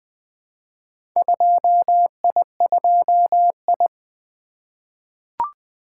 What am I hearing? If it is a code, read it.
Morse code, heard as a keyed tone, about 20 wpm, 700 Hz. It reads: 2I2I
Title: Four Letter-Number Combinations - 20wpm